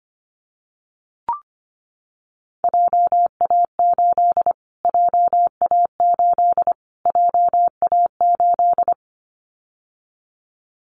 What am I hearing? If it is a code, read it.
Morse code, heard as a keyed tone, about 25 wpm, 700 Hz. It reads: JA8 JA8 JA8
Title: Callsign Prefixes for Top 25 DXCC - Repeated 3x - 25wpm